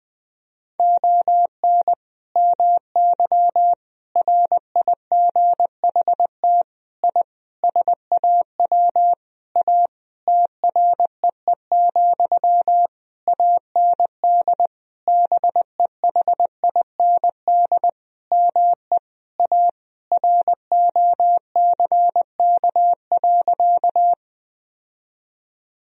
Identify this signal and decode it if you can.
Morse code, heard as a keyed tone, about 21 wpm, 700 Hz. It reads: ON MY RIGHT I SAW A TREE, AND BEHIND ME A ROCK.
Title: Sentences from Top 1000 Words - Review Part 2 - 20wpm